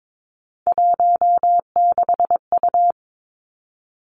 Morse code, heard as a keyed tone, about 22 wpm, 700 Hz. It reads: 16U
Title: Three Challenging Letter-Number Combinations 22wpm